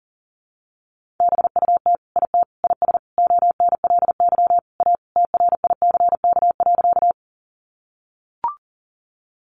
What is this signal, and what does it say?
Morse code, heard as a keyed tone, about 40 wpm, 700 Hz. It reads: BUT IT IS ONLY A TRICK.
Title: Sentences from Top 1500 Words 40wpm